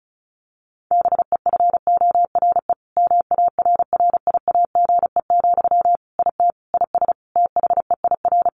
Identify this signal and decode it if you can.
Morse code, heard as a keyed tone, about 34 wpm, 700 Hz. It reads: BEFORE MARRIAGE, IT IS THEIR
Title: Sentences from Top 1500 Words 35wpm